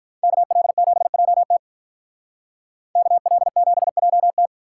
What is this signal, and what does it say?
Morse code, heard as a keyed tone, about 43 wpm, 700 Hz. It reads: KL7JT KL7JT
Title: Speed Racing - USA 2x2 Call Signs 35wpm